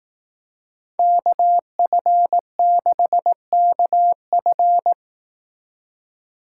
Morse code, heard as a keyed tone, about 18 wpm, 700 Hz. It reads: KF6KF
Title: Speed Racing - USA 2x2 Call Signs 15wpm